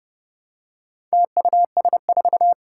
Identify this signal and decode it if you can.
Morse code, heard as a keyed tone, about 30 wpm, 700 Hz. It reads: TUS4
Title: Four Challenging Letter-Number Combinations 30wpm